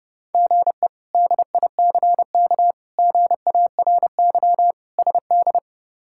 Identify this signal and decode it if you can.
Morse code, heard as a keyed tone, about 30 wpm, 700 Hz. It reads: GE DICK GARY SD
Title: K1USN SST QSO Practice - 30wpm